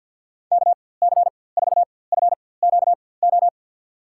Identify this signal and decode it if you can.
Morse code, heard as a keyed tone, about 52 wpm, spaced as for 24 wpm, 700 Hz. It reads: KC4FQO
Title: USA Call Signs - 6 Characters - 4x Character Spacing - 50wpm